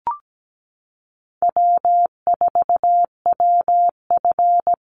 Morse code, heard as a keyed tone, about 17 wpm, 700 Hz. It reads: W4WF
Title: Call Signs - 17wpm